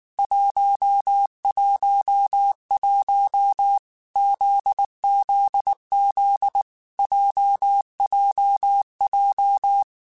Morse code, heard as a keyed tone, about 19 wpm, 800 Hz. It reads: 111 ZZZ JJJ
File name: Basic Sets / Basic Letter Groups Collective7 / JZ17@18